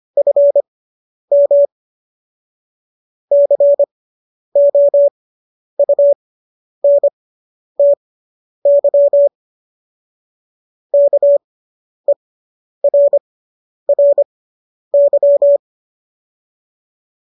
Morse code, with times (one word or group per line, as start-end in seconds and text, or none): 0.2-1.6 FM
3.3-9.3 COUNTY
10.9-15.6 KERRY